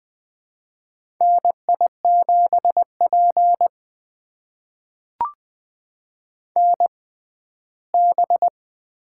Morse code, heard as a keyed tone, about 20 wpm, spaced as for 14 wpm, 700 Hz. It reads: NI7P NB
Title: USA Call Signs - 4 Characters - 6x Character Spacing - 20wpm